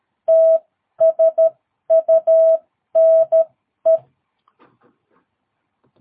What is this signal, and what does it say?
Morse code, heard as a keyed tone, about 13 wpm, spaced as for 11 wpm, 660 Hz. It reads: TSUNE
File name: jh8cbh / 2022cw